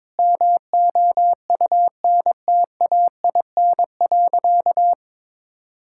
Morse code, heard as a keyed tone, about 22 wpm, 700 Hz. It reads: MOUNTAIN.
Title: Sentences from Top 1000 Words - Review Part 5 - 22wpm